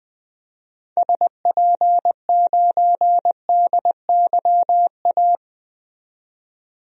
Morse code, heard as a keyed tone, about 20 wpm, 700 Hz. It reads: SP9DYA
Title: DXCC-100 Call Signs - 6 Characters - 20wpm